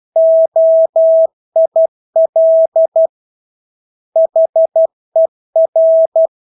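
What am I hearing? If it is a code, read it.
Morse code, heard as a keyed tone, about 12 wpm, 650 Hz. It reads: OIL HER
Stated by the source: Morse practice of English words at 12 wpm.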